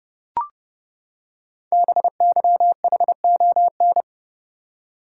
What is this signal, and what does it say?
Morse code, heard as a keyed tone, about 30 wpm, 700 Hz. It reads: BYHON
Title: Code Groups - 5 Letters - 30wpm